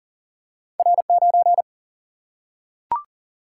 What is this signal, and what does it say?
Morse code, heard as a keyed tone, about 40 wpm, 700 Hz. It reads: R 9
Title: Single-Letter Single-Number 40wpm